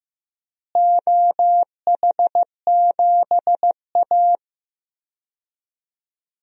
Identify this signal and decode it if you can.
Morse code, heard as a keyed tone, about 16 wpm, 700 Hz. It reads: OH7A
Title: DXCC-100 Call Signs - 4 Characters - Repeated 3x - 15wpm